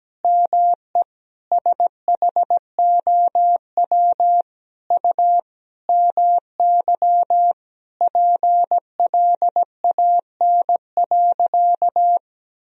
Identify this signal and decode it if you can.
Morse code, heard as a keyed tone, about 17 wpm, 700 Hz. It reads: ME SHOW U MY PLAN.
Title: Sentences from Top 500 Words and Abbreviations 17wpm